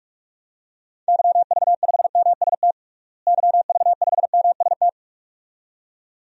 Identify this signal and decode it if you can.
Morse code, heard as a keyed tone, about 45 wpm, 700 Hz. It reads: YV5MST YV5MST
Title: DXCC-100 Call Signs - 6 Characters - Repeated 2x - 45wpm